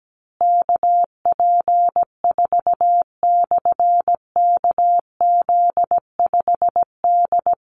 Morse code, heard as a keyed tone, about 17 wpm, 700 Hz. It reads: KP4/KZ5D
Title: Call Signs Repeated 2x - 17wpm